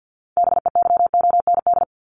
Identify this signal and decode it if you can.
Morse code, heard as a keyed tone, about 50 wpm, 700 Hz. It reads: BEYOND
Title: Top 751-1000 Words 50wpm